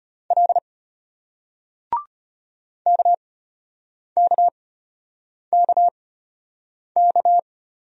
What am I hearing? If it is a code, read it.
Morse code, heard as a keyed tone, about 34 wpm, spaced as for 8 wpm, 700 Hz. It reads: L KKKK